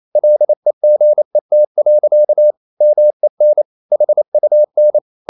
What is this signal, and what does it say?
Morse code, heard as a keyed tone, about 28 wpm, 600 Hz. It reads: LEGET. MEN HUN
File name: Ferie QRQ 28wpm